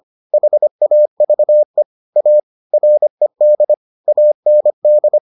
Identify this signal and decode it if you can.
Morse code, heard as a keyed tone, about 25 wpm, 600 Hz. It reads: HAVE A RED AND